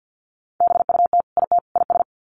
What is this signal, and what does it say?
Morse code, heard as a keyed tone, about 48 wpm, 700 Hz. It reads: BUT IT IS
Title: Sets of 3 Words 50wpm